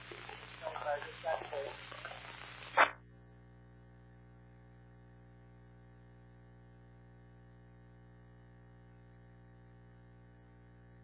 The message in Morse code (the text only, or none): none